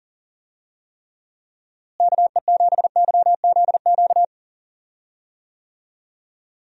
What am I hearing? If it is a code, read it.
Morse code, heard as a keyed tone, about 40 wpm, 700 Hz. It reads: KE7YZQ
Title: USA Call Signs - 6 Characters - Repeated 2x - 40wpm